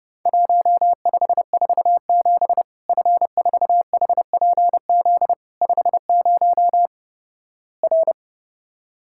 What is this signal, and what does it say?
Morse code, heard as a keyed tone, about 30 wpm, 700 Hz. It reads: 1547 F4HPZ 50 R
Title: Worked All Europe DX Contest - QTC Exchange - 30wpm